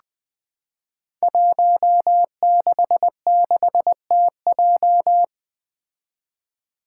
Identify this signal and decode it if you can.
Morse code, heard as a keyed tone, about 20 wpm, 700 Hz. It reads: 166TJ
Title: Five Challenging Letter-Number Combinations 20wpm